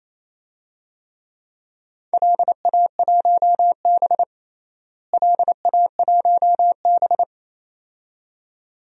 Morse code, heard as a keyed tone, about 28 wpm, 700 Hz. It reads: LA1B LA1B
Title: DXCC-100 Call Signs - 4 Characters - Repeated 2x - 28wpm